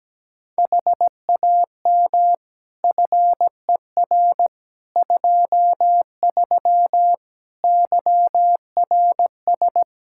Morse code, heard as a keyed tone, about 17 wpm, 700 Hz. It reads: HAM FER 23 YRS
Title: Practice QSO - Exchange 3 - More Info - 17wpm